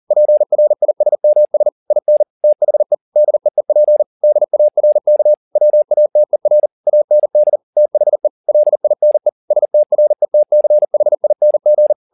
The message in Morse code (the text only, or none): PRISMS IN THE DEEP DARK WATER AND THE LINE STRETCHING